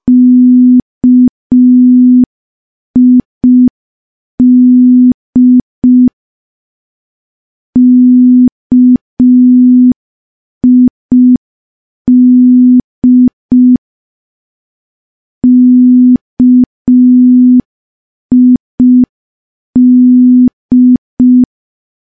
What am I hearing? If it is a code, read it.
Morse code, heard as a keyed tone, about 5 wpm, 250 Hz. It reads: KID KID KID
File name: morse / morse-training-05WPM